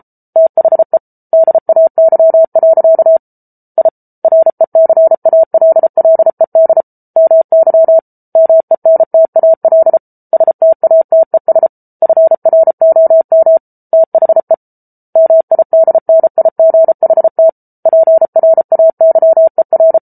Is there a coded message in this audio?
Yes: Morse code from 0.4 to 20.0 s, heard as a keyed tone, about 34 wpm, 650 Hz. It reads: THE DAY. I RECALLED MY MENTAL STATES FROM THE MIDNIGHT PRAYER